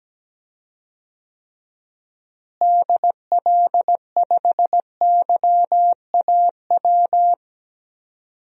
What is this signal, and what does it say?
Morse code, heard as a keyed tone, about 17 wpm, 700 Hz. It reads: DL5YAW
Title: DXCC-100 Call Signs - 6 Characters - Repeated 3x - 17wpm